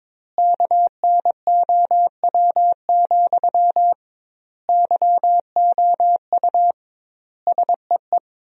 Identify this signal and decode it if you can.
Morse code, heard as a keyed tone, about 22 wpm, 700 Hz. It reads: KNOW, YOU SEE
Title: 2x Word-Spacing - Sentences from Top 200 Words - 22wpm